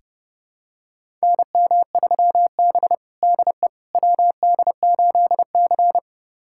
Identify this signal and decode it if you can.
Morse code, heard as a keyed tone, about 32 wpm, 700 Hz. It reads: NM3B DE WD8C
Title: USA Callsign DE Callsign - 4 Characters - 30wpm